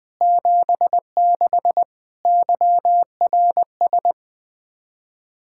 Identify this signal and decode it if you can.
Morse code, heard as a keyed tone, about 20 wpm, 700 Hz. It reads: 76 YRS
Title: K1USN SST QSO Practice 20wpm